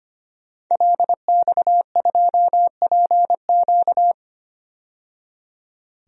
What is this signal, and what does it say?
Morse code, heard as a keyed tone, about 25 wpm, 700 Hz. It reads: LX2PQ
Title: DXCC-100 Call Signs - 5 Characters - 25wpm